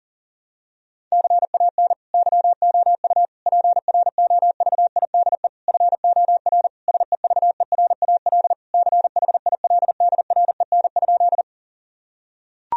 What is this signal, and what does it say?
Morse code, heard as a keyed tone, about 37 wpm, 700 Hz. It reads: CAN YOU PROVIDE FOR SEVERAL CHILDREN?
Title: Sentences from Top 1000 Words - Review Part 2 - 40wpm